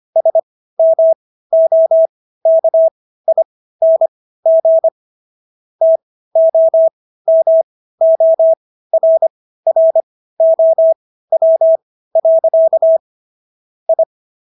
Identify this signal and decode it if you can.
Morse code, heard as a keyed tone, about 25 wpm, spaced as for 15 wpm, 650 Hz. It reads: SMOKING TOMORROW. I